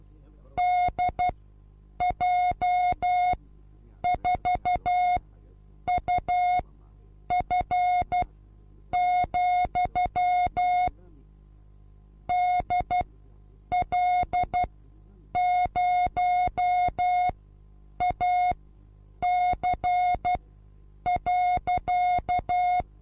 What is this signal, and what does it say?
Morse code, heard as a keyed tone, about 12 wpm, spaced as for 8 wpm, 720 Hz. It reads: DJ4UF, DL0AC.
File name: Onlinelehrgang / morsen / DJ4UF Morsen23